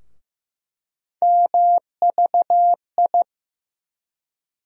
Morse code, heard as a keyed tone, about 15 wpm, 700 Hz. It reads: MVI
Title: USA Call Sign Postfixes - 3 Letters 15wpm